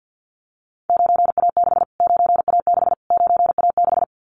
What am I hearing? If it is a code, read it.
Morse code, heard as a keyed tone, about 50 wpm, 700 Hz. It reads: 9A6 9A6 9A6
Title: Callsign Prefixes for Top 25 DXCC - Repeated 3x - 50wpm